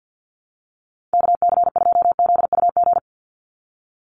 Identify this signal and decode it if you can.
Morse code, heard as a keyed tone, about 50 wpm, 700 Hz. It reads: KC2ZUG
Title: USA Call Signs 50wpm